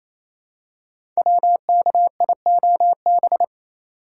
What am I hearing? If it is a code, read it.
Morse code, heard as a keyed tone, about 29 wpm, 700 Hz. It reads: WKIOB
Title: Code Groups - 5 Letters - 28wpm